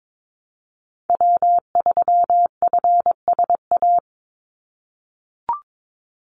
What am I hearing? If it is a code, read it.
Morse code, heard as a keyed tone, about 22 wpm, 700 Hz. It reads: W3FSA
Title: Call Signs - 22wpm